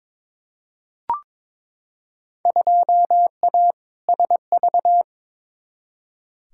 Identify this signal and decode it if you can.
Morse code, heard as a keyed tone, about 22 wpm, 700 Hz. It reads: 2A SV
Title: Field Day Exchanges - 22wpm